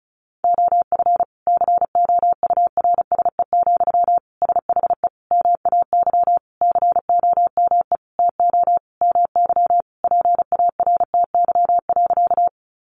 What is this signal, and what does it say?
Morse code, heard as a keyed tone, about 34 wpm, 700 Hz. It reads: OF COURSE, SHE MAY COME TO MY PARTY.
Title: Sentences from Top 1400 Words 35wpm